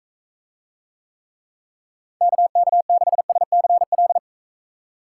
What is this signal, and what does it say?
Morse code, heard as a keyed tone, about 40 wpm, 700 Hz. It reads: KK6SCL